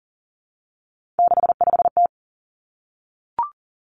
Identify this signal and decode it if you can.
Morse code, heard as a keyed tone, about 40 wpm, 700 Hz. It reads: 65T
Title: Three Challenging Letter-Number Combinations 40wpm